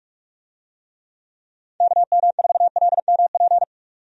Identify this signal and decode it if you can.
Morse code, heard as a keyed tone, about 46 wpm, 700 Hz. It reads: KM4LMP
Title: USA Call Signs - 6 Characters - 45wpm